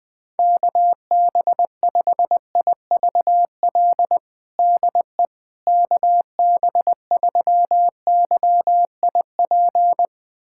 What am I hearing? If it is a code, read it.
Morse code, heard as a keyed tone, about 21 wpm, 700 Hz. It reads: KB5IVL DE KB3YIP
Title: USA Callsign DE Callsign - 6 Characters - 20wpm